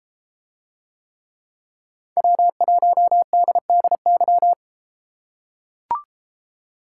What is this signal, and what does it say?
Morse code, heard as a keyed tone, about 32 wpm, 700 Hz. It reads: W1DDY